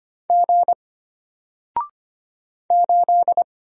Morse code, heard as a keyed tone, about 25 wpm, 700 Hz. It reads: G 8
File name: Single-Letter-Number-25wpm